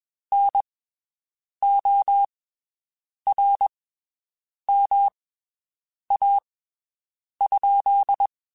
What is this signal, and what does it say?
Morse code, heard as a keyed tone, about 21 wpm, spaced as for 7 wpm, 800 Hz. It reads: NORMA?
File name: (spaced) / QSOs Spaced-mp3s / WD0FBT 1 @20 sp